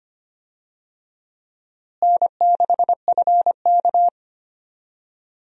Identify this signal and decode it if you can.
Morse code, heard as a keyed tone, about 25 wpm, 700 Hz. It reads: N6FK
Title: USA Call Signs - 4 Characters - Repeated 3x - 25wpm